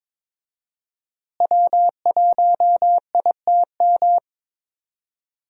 Morse code, heard as a keyed tone, about 21 wpm, 700 Hz. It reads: W1ITM
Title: USA Call Signs - 5 Characters - Repeated 2x - 22wpm